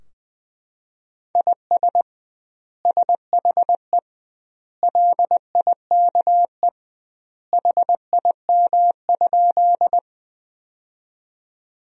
Morse code, heard as a keyed tone, about 21 wpm, 700 Hz. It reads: IS SHE LIKE HIM?
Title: 2x Word-Spacing - Sentences from Top 100 Words - 20wpm